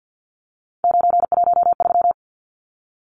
Morse code, heard as a keyed tone, about 50 wpm, 700 Hz. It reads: 913
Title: Numbers - 3-Digit 50wpm